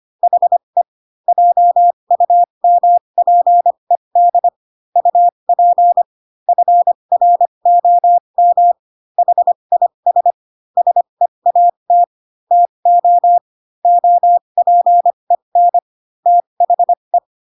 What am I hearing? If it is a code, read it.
Morse code, heard as a keyed tone, about 25 wpm, spaced as for 22 wpm, 700 Hz. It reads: HE JUMPED UP FROM HIS SEAT TO OPEN THE
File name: Pinocchio / Pinocchio ch7 18 25 ik1hge